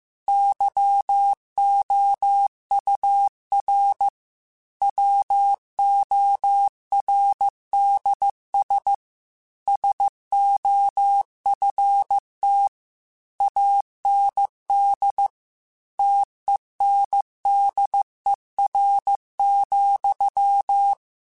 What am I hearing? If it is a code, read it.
Morse code, heard as a keyed tone, about 15 wpm, 800 Hz. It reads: YOUR WORDS SOFT AND TENDER,
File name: Chuckles / Chuckles-mp3s / SoftWords @13